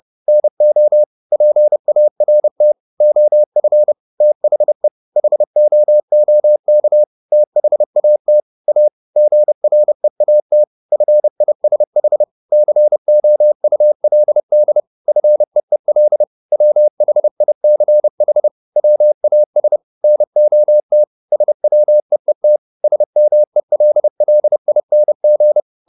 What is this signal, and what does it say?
Morse code, heard as a keyed tone, about 30 wpm, 600 Hz. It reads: NO PART OF THE HOOK THAT A GREAT FISH COULD FEEL WHICH WAS NOT SWEET SMELLING
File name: the old man and the sea / the old man and the sea 1 30wpm